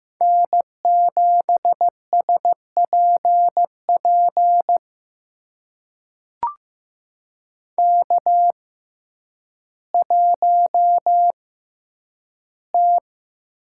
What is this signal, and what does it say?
Morse code, heard as a keyed tone, about 15 wpm, spaced as for 9 wpm, 700 Hz. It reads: N7SPP K1T
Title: USA Call Signs - 5 Characters - 6x Character Spacing - 15wpm